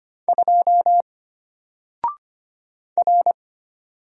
Morse code, heard as a keyed tone, about 25 wpm, 700 Hz. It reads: 2 R
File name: Single-Letter-Number-25wpm